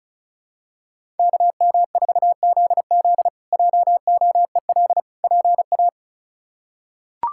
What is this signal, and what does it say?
Morse code, heard as a keyed tone, about 36 wpm, 700 Hz. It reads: KM4ZZ JOEL PA
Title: K1USN SST QSO Practice - 35wpm